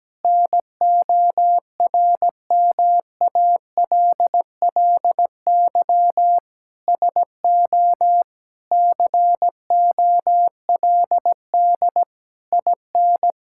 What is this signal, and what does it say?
Morse code, heard as a keyed tone, about 17 wpm, 700 Hz. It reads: NORMALLY SO COLD IN
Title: Sentences from Top 1000 Words - Review Part 6 - 17wpm